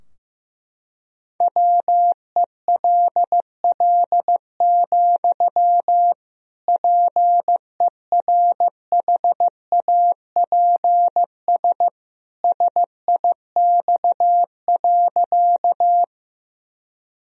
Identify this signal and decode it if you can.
Morse code, heard as a keyed tone, about 15 wpm, 700 Hz. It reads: WELL, PERHAPS SIX.
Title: Sentences from Top 1000 Words - Review Part 2 - 15wpm